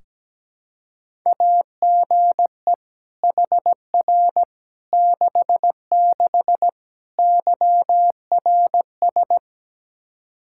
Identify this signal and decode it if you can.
Morse code, heard as a keyed tone, about 17 wpm, 700 Hz. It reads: AGE HR 66 YRS
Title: Practice QSO - Exchange 3 - More Info - 17wpm